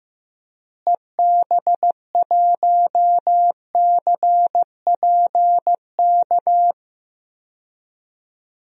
Morse code, heard as a keyed tone, about 15 wpm, 700 Hz. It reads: EB1CPK
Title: DXCC-100 Call Signs - 6 Characters - Repeated 3x - 15wpm